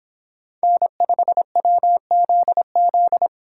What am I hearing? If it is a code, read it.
Morse code, heard as a keyed tone, about 24 wpm, 700 Hz. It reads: N5WZZ